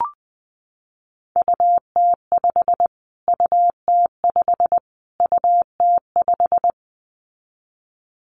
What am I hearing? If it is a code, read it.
Morse code, heard as a keyed tone, about 19 wpm, 700 Hz. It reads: UT5 UT5 UT5
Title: Callsign Prefixes for Top 25 DXCC - Repeated 3x - 20wpm